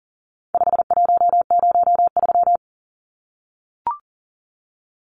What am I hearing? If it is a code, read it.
Morse code, heard as a keyed tone, about 38 wpm, 700 Hz. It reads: 5103
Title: Numbers - 4-Digit 40wpm